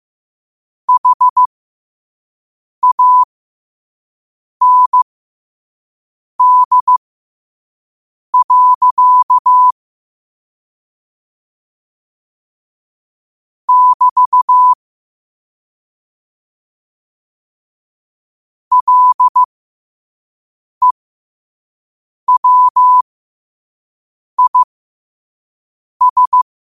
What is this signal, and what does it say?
Morse code, heard as a keyed tone, about 15 wpm, spaced as for 5 wpm, 1000 Hz. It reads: HAND. <BT> LEWIS
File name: qotdfiles / 5wpm